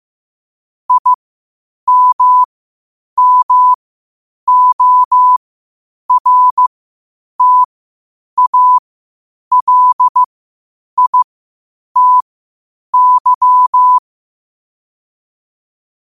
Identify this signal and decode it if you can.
Morse code, heard as a keyed tone, about 15 wpm, spaced as for 8 wpm, 1000 Hz. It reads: IMMORTALITY
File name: qotdfiles / 8wpm